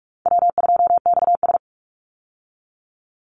Morse code, heard as a keyed tone, about 44 wpm, 700 Hz. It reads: W2XS
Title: Easy Call Signs - 45wpm